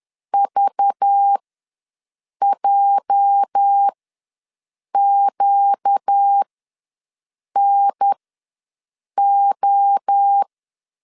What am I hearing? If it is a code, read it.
Morse code, heard as a keyed tone, about 10 wpm, spaced as for 6 wpm, 800 Hz. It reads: VJQNO
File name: LEZIONI / 8 YJ45 / 8 YJ ws 10wpm 4043